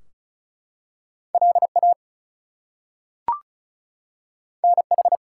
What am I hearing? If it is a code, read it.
Morse code, heard as a keyed tone, about 36 wpm, 700 Hz. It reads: LA NH
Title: US States and Canadian Provinces 35wpm